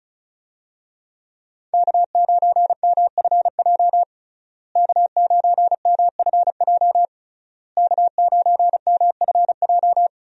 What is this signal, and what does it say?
Morse code, heard as a keyed tone, about 34 wpm, 700 Hz. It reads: K9MFJ K9MFJ K9MFJ
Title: USA Call Signs - 5 Characters - Repeated 3x - 35wpm